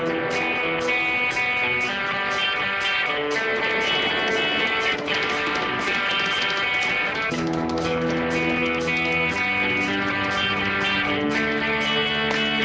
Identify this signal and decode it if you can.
Dial-up modem handshake.